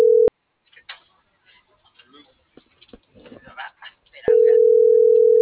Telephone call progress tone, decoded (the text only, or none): ringback tone